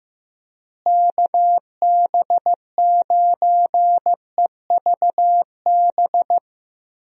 Morse code, heard as a keyed tone, about 16 wpm, 700 Hz. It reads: KB9EVB
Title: USA Call Signs - 6 Characters - Repeated 3x - 15wpm